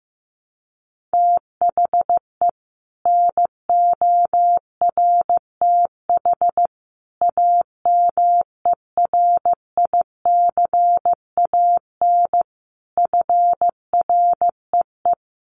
Morse code, heard as a keyed tone, about 15 wpm, 700 Hz. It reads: THE NORTH AMERICAN FREE
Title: Sets of 5 Words Encore 15wpm